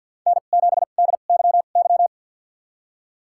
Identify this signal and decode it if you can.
Morse code, heard as a keyed tone, about 51 wpm, spaced as for 36 wpm, 700 Hz. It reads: N7DYY
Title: USA Call Signs - 5 Characters - 2x Character Spacing - 50wpm